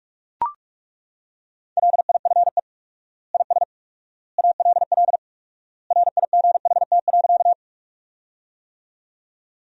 Morse code, heard as a keyed tone, about 41 wpm, 700 Hz. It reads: LIFE IS ALL RIGHT.
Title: 4x Word-Spacing - Sentences from Top 300 Words - 45wpm